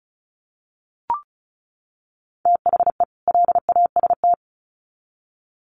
Morse code, heard as a keyed tone, about 34 wpm, 700 Hz. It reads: THE LAST
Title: Sets of 2 Words 35wpm